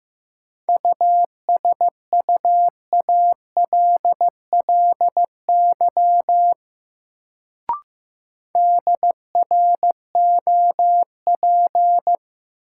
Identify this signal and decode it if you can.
Morse code, heard as a keyed tone, about 15 wpm, 700 Hz. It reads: USUALLY DROP
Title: Sentences from Top 1400 Words - New Word Drill Alt - 15wpm